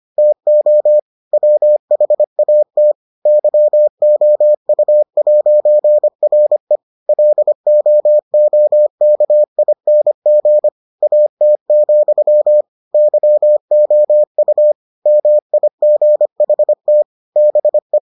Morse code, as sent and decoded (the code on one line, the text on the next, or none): - --- / .-- .... .- - / -.-- --- ..- .----. .-. / .-.. --- --- -.- .. -. --. / .- - --..-- / -.-- --- ..- / -- .. --. .... - / -... .
TO WHAT YOU'RE LOOKING AT, YOU MIGHT BE